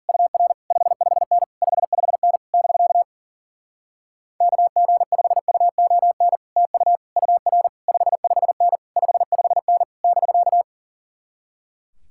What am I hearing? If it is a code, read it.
Morse code, heard as a keyed tone, about 50 wpm, 700 Hz. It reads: UR 55N 55N <BK> KG5UON TU UR 55N 55N <BK>
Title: Speed-Racing Common QSO Phrases Encore - 40wpm